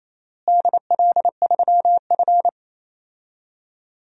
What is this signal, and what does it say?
Morse code, heard as a keyed tone, about 28 wpm, 700 Hz. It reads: DL3F
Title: DXCC-100 Call Signs - 4 Characters - Repeated 2x - 28wpm